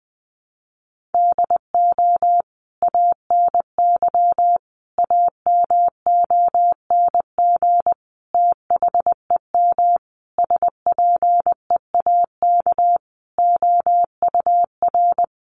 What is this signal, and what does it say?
Morse code, heard as a keyed tone, about 20 wpm, 700 Hz. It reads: DO ANY AMONG THEM SPEAK OUR
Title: Sentences from Top 1100 Words 20wpm